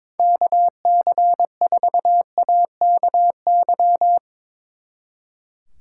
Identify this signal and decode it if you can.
Morse code, heard as a keyed tone, about 23 wpm, 700 Hz. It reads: KC4AKY
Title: Speed Racing - USA 2x3 Call Signs 22wpm